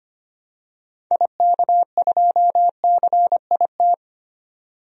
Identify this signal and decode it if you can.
Morse code, heard as a keyed tone, about 25 wpm, 700 Hz. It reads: IK2CIT
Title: DXCC-100 Call Signs - 6 Characters - 25wpm